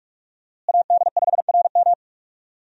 Morse code, heard as a keyed tone, about 42 wpm, 700 Hz. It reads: AD5RM
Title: USA Call Signs - 5 Characters - Repeated 2x - 45wpm